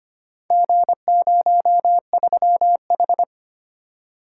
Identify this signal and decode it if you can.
Morse code, heard as a keyed tone, about 25 wpm, 700 Hz. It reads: G03H
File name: practice / 25wpm / Four Letter-Number Combinations - 25wpm